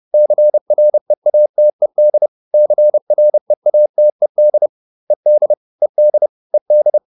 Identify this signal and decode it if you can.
Morse code, heard as a keyed tone, about 30 wpm, 600 Hz. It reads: CREATED CREATED ED ED ED